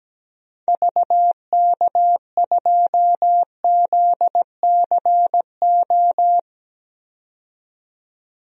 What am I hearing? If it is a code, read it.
Morse code, heard as a keyed tone, about 17 wpm, 700 Hz. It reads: VK2ZCO
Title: DXCC-100 Call Signs - 6 Characters - 17wpm